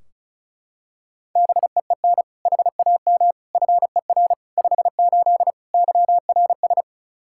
Morse code, heard as a keyed tone, about 36 wpm, 700 Hz. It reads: BEEN HAM FER 58 YRS
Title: Practice QSO - Exchange 3 - More Info - 35wpm